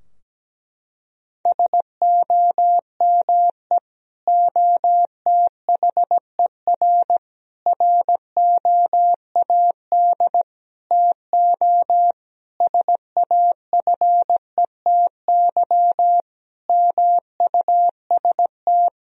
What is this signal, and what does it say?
Morse code, heard as a keyed tone, about 17 wpm, 700 Hz. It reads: SOME OTHER ROAD TO SAFETY MUST
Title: Sentences from Top 1000 Words - Review Part 4 - 17wpm